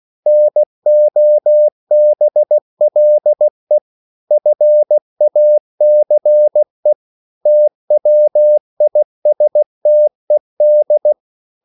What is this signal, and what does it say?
Morse code, heard as a keyed tone, about 16 wpm, 600 Hz. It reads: NOBLE FACE TWISTED